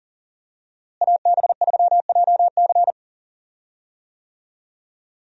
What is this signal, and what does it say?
Morse code, heard as a keyed tone, about 40 wpm, 700 Hz. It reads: AB3JC
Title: USA Call Signs - 5 Characters - Repeated 2x - 40wpm